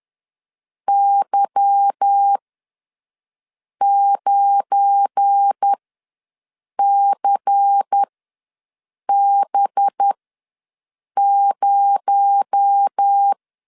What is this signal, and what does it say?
Morse code, heard as a keyed tone, about 10 wpm, spaced as for 6 wpm, 800 Hz. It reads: Y9CB0
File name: LEZIONI / 9 28 ws 10wpm 4124